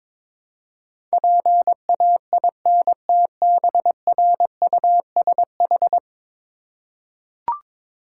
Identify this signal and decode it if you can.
Morse code, heard as a keyed tone, about 22 wpm, 700 Hz. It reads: PAINTBRUSH
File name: Difficult Words with Hints - 22wpm